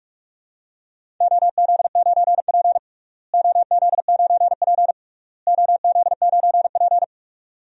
Morse code, heard as a keyed tone, about 46 wpm, 700 Hz. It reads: OZ9P OZ9P OZ9P
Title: DXCC-100 Call Signs - 4 Characters - Repeated 3x - 45wpm